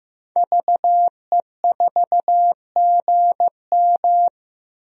Morse code, heard as a keyed tone, about 16 wpm, 700 Hz. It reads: VE4GM
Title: DXCC-100 Call Signs - 5 Characters - Repeated 3x - 15wpm